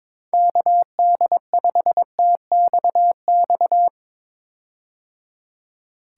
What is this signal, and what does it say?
Morse code, heard as a keyed tone, about 22 wpm, 700 Hz. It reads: KD5TXX
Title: K1USN SST QSO Practice - 22wpm